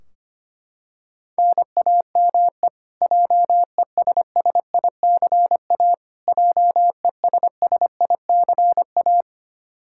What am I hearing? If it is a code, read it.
Morse code, heard as a keyed tone, about 25 wpm, 700 Hz. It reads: NAME JESSICA JESSICA
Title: Practice QSO - Exchange 1 - Beginning - 25wpm